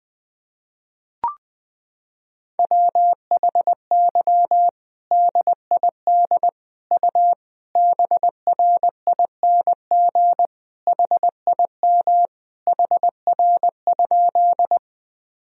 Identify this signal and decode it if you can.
Morse code, heard as a keyed tone, about 20 wpm, 700 Hz. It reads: WHY DID U BRING HIM HR?
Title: Sentences from Top 500 Words and Abbreviations 20wpm